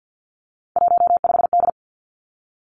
Morse code, heard as a keyed tone, about 50 wpm, 700 Hz. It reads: 15D
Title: Three Challenging Letter-Number Combinations 50wpm